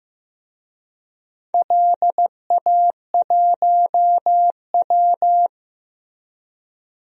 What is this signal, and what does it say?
Morse code, heard as a keyed tone, about 16 wpm, 700 Hz. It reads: LA1W
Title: DXCC-100 Call Signs - 4 Characters - Repeated 3x - 15wpm